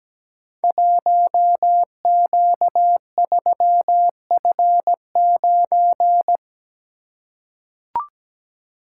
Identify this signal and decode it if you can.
Morse code, heard as a keyed tone, about 17 wpm, 700 Hz. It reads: 1Q3F9
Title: Code Groups - 5 Letter:Numbers - 17wpm